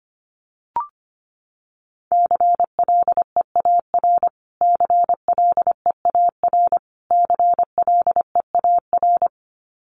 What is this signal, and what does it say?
Morse code, heard as a keyed tone, about 25 wpm, 700 Hz. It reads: CLEAR CLEAR CLEAR
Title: Top 500 Words Repeated 25wpm